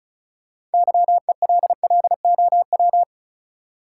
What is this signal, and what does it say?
Morse code, heard as a keyed tone, about 36 wpm, 700 Hz. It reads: YELLOW
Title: Six Letter Words - 35wpm